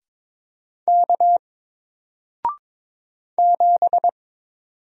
Morse code, heard as a keyed tone, about 22 wpm, 700 Hz. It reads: K 7
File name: Single-Letter-Number-22wpm